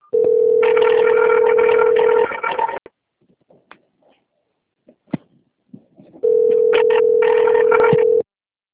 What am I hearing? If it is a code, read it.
Ringback tone (North America), 440 + 480 Hz.
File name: out- - - - - .44136